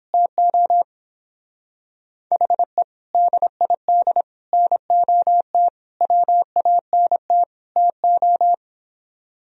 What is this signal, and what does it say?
Morse code, heard as a keyed tone, about 31 wpm, 700 Hz. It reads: TO HE DID NOT WANT TO